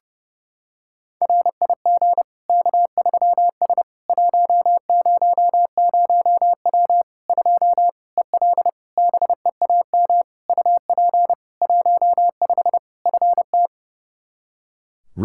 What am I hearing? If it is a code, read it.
Morse code, heard as a keyed tone, about 30 wpm, 700 Hz. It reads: RIG K3S 100W 2 EL BEAM UP 15 FT
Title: Practice QSO - Exchange 2 - Info - 30wpm